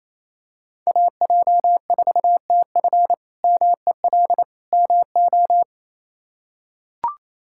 Morse code, heard as a keyed tone, about 28 wpm, 700 Hz. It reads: AJ4TF MEL MO
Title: K1USN SST QSO Practice - 28wpm